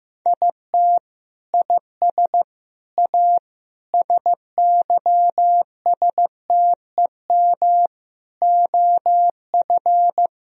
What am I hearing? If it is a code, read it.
Morse code, heard as a keyed tone, about 15 wpm, 700 Hz. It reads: IT IS A SYSTEM OF
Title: Sentences from Top 1000 Words - Review Part 4 - 15wpm